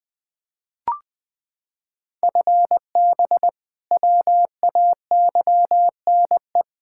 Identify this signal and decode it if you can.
Morse code, heard as a keyed tone, about 20 wpm, 700 Hz. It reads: FB WAYNE
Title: K1USN SST QSO Practice 20wpm